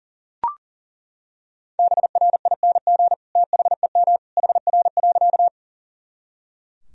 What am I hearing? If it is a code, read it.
Morse code, heard as a keyed tone, about 40 wpm, 700 Hz. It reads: BRING THEM HR.
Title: Sentences from Top 500 Words and Abbreviations 40wpm